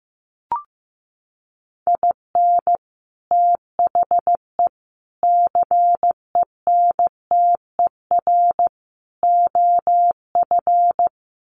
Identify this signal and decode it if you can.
Morse code, heard as a keyed tone, about 15 wpm, 700 Hz. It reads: IN THE CENTER OF
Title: Sets of 4 Words 15wpm